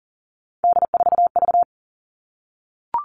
Morse code, heard as a keyed tone, about 40 wpm, 700 Hz. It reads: D4V
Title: All Cut Numbers - 3-Digit 40wpm